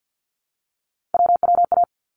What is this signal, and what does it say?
Morse code, heard as a keyed tone, about 50 wpm, 700 Hz. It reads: WWA
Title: ARRL Sections 50wpm